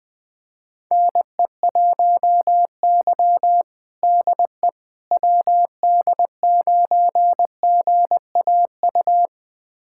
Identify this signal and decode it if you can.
Morse code, heard as a keyed tone, about 20 wpm, 700 Hz. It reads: NE1Y DE WD9GAU
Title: USA Callsign DE Callsign 20wpm